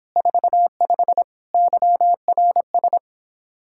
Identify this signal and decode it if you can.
Morse code, heard as a keyed tone, about 29 wpm, 700 Hz. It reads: 45 YRS